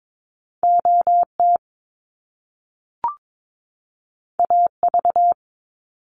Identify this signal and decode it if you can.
Morse code, heard as a keyed tone, about 22 wpm, 700 Hz. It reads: OT AV
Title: Top Two Letter Combinations in Words 22wpm